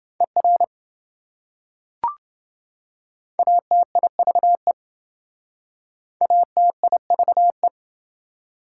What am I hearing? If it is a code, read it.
Morse code, heard as a keyed tone, about 27 wpm, 700 Hz. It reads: ER ATIVE ATIVE